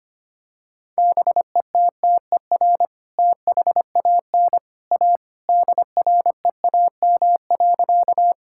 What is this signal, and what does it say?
Morse code, heard as a keyed tone, about 25 wpm, 700 Hz. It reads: BETTER THAN A DREAM.
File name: practice / 25wpm / Sentences with 2-Word Phrases - 25wpm